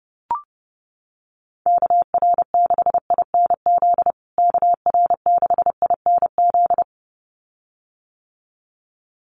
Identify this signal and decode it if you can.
Morse code, heard as a keyed tone, about 30 wpm, 700 Hz. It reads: KR6INZ KR6INZ
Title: USA Call Signs Repeated 2x 30wpm